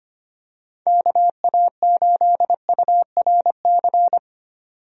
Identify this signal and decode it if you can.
Morse code, heard as a keyed tone, about 25 wpm, 700 Hz. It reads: KA8URC
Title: USA Call Signs - 6 Characters - 25wpm